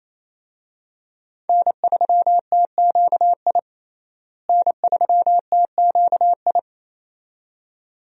Morse code, heard as a keyed tone, about 28 wpm, 700 Hz. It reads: N3TQI N3TQI
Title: USA Call Signs - 5 Characters - Repeated 2x - 28wpm